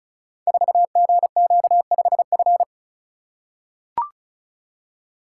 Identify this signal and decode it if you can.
Morse code, heard as a keyed tone, about 36 wpm, 700 Hz. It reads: 4GQ5F
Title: Code Groups - 5 Letter:Numbers - 35wpm